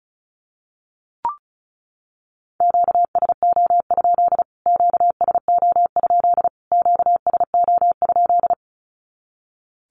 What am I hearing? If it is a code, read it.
Morse code, heard as a keyed tone, about 34 wpm, 700 Hz. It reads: QSO? QSO? QSO?
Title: Q-Codes Repeated 3x 35wpm